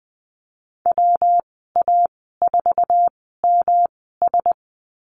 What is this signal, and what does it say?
Morse code, heard as a keyed tone, about 20 wpm, spaced as for 15 wpm, 700 Hz. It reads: WA4MS
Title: 207 Lesson 33g - Letter j - Call Signs 20wpm